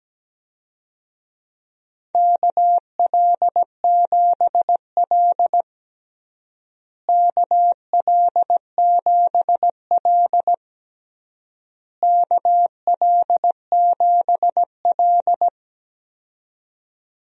Morse code, heard as a keyed tone, about 17 wpm, 700 Hz. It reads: KL7L KL7L KL7L
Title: DXCC-100 Call Signs - 4 Characters - Repeated 3x - 17wpm